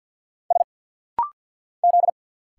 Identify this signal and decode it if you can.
Morse code, heard as a keyed tone, about 49 wpm, 700 Hz. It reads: S Z
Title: Single Letter-Number Alt - Rapid-Fire 50wpm